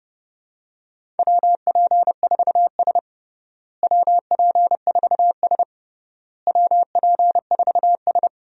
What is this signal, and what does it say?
Morse code, heard as a keyed tone, about 29 wpm, 700 Hz. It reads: WP4S WP4S WP4S
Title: USA Call Signs - 4 Characters - Repeated 3x - 30wpm